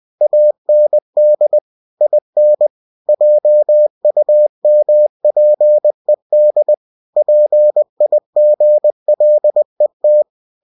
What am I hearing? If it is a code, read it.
Morse code, heard as a keyed tone, about 20 wpm, 600 Hz. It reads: AND IN JUMPED PIGLET
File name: cw-books / winnie-the-pooh / winnie-20wpm-page-0069